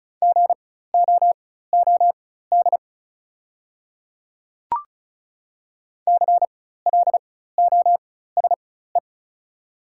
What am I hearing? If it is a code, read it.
Morse code, heard as a keyed tone, about 34 wpm, spaced as for 16 wpm, 700 Hz. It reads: GOOD CLOSE
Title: 4x Character Spacing - Top 500 Words - 35wpm